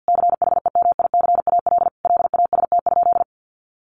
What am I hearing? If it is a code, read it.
Morse code, heard as a keyed tone, about 48 wpm, 700 Hz. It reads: CHEMICAL LAST?
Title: Sentences from Top 1200 Words 50wpm